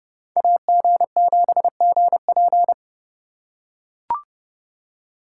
Morse code, heard as a keyed tone, about 30 wpm, 700 Hz. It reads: AG7GP